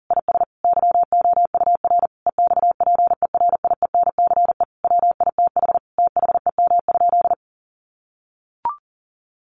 Morse code, heard as a keyed tone, about 40 wpm, 700 Hz. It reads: IS YOUR EXPERIENCE WITH THEM?
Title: Sentences from Top 600 Words 40wpm